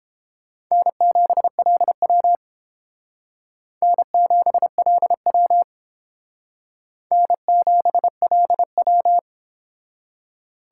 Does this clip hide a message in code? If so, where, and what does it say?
Yes: Morse code from 0.7 to 9.2 s, heard as a keyed tone, about 29 wpm, 700 Hz. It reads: N7LW N7LW N7LW